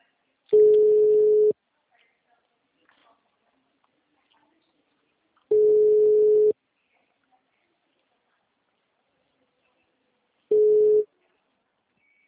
Ringback tone (Europe, 425 Hz).
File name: out- -2005- - - .73340